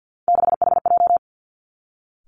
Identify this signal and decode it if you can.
Morse code, heard as a keyed tone, about 50 wpm, 700 Hz. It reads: 6HJ
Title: Three Challenging Letter-Number Combinations 50wpm